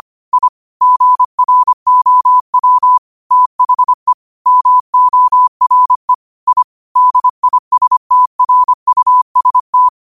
Morse code, heard as a keyed tone, about 25 wpm, 1000 Hz. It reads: I GROW THE MORE I DISTRUST
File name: qotdfiles / 25wpm